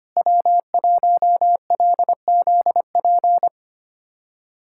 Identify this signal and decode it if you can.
Morse code, heard as a keyed tone, about 25 wpm, 700 Hz. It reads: W1LZP
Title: USA Callsign DE Callsign 25wpm